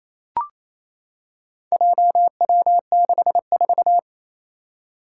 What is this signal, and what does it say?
Morse code, heard as a keyed tone, about 28 wpm, 700 Hz. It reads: JW64
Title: Four Challenging Letter-Number Combinations 28wpm